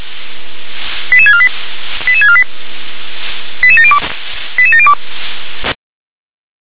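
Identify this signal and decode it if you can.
Five-tone selective call (ZVEI), heard as quick runs of five tones, tones of about 70 ms. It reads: 88648 88648 88702 88702
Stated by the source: ZVEI five-tone call 88730 received on SDR.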